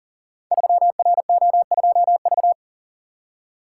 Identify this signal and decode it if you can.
Morse code, heard as a keyed tone, about 39 wpm, 690 Hz. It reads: 3RO2V
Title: Code Groups - 5 Letter:Numbers - 40wpm